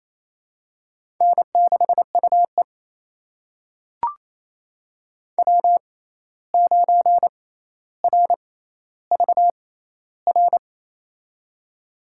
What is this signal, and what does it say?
Morse code, heard as a keyed tone, about 28 wpm, spaced as for 13 wpm, 700 Hz. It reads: N6UE W9RVR
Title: USA Call Signs - 6x Character Spacing - 28wpm